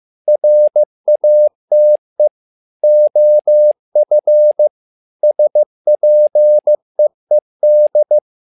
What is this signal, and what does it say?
Morse code, heard as a keyed tone, about 15 wpm, 600 Hz. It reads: RATE OF SPEED
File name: learning center / tower trouble / tower trouble 15wpm